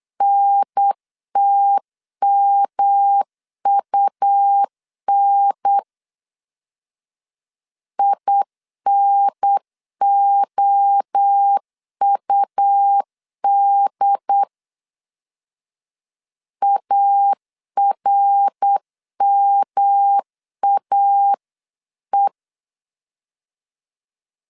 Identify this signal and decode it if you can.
Morse code, heard as a keyed tone, about 8 wpm, 800 Hz. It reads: NTMUN INOUD ARMAE